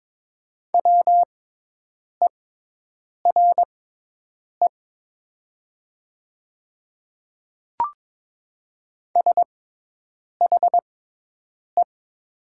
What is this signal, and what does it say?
Morse code, heard as a keyed tone, about 22 wpm, spaced as for 8 wpm, 700 Hz. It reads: WERE SHE
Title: 6x Character Spacing - Top 100 Words - 22wpm